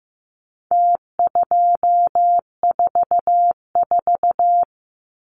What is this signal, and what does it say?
Morse code, heard as a keyed tone, about 15 wpm, 700 Hz. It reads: T244
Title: Numbers - Contest Style - 4-Digit 15wpm